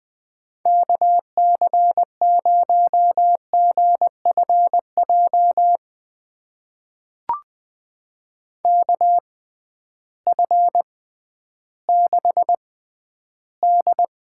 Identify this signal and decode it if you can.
Morse code, heard as a keyed tone, about 20 wpm, spaced as for 12 wpm, 700 Hz. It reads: KC0GFJ KF6D
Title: USA Call Signs - 6 Characters - 6x Character Spacing - 20wpm